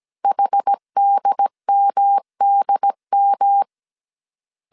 Morse code, heard as a keyed tone, about 17 wpm, 800 Hz. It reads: HDMDM